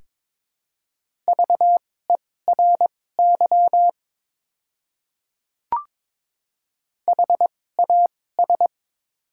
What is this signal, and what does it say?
Morse code, heard as a keyed tone, about 23 wpm, spaced as for 16 wpm, 700 Hz. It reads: VERY HAS
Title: 2x Character Spacing - Top 200 Words - 22wpm